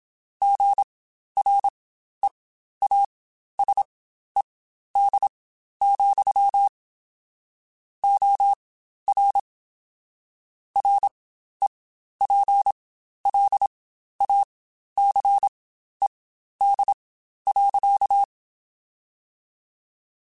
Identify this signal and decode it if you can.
Morse code, heard as a keyed tone, about 27 wpm, spaced as for 12 wpm, 800 Hz. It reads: GREASED, OR REPLACED.
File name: Spaced / Chuckles Spaced-mp3s / Squeaky sp @25